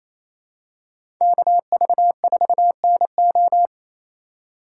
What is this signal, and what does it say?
Morse code, heard as a keyed tone, about 28 wpm, 700 Hz. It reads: KV4NO
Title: USA Call Signs - 5 Characters - 4x Character Spacing - 28wpm